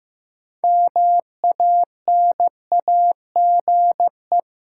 Morse code, heard as a keyed tone, about 15 wpm, 700 Hz. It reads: MANAGE
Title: Sentences from Top 1400 Words - New Word Drill Alt - 15wpm